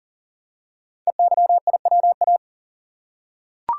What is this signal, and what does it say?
Morse code, heard as a keyed tone, about 40 wpm, 700 Hz. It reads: EYIWA
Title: Code Groups - 5 Letters - 40wpm